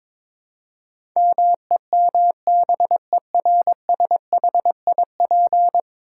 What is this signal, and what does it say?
Morse code, heard as a keyed tone, about 22 wpm, 700 Hz. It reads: MEMBERSHIP
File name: practice / 22wpm / 10 Letter Words - First 3 Letters Given - 22wpm